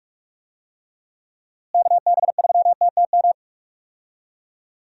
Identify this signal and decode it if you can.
Morse code, heard as a keyed tone, about 46 wpm, 700 Hz. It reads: KB3TTM
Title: USA Call Signs - 6 Characters - Repeated 3x - 45wpm